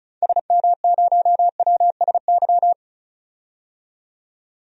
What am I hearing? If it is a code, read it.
Morse code, heard as a keyed tone, about 36 wpm, 700 Hz. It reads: SM0WSY
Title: DXCC-100 Call Signs - 6 Characters - 35wpm